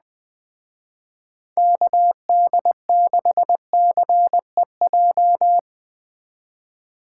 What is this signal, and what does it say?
Morse code, heard as a keyed tone, about 19 wpm, 700 Hz. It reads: KD6CEJ